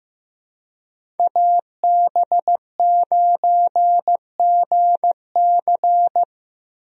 Morse code, heard as a keyed tone, about 16 wpm, 700 Hz. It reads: AB9GC
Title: USA Call Signs - 5 Characters - Repeated 2x - 15wpm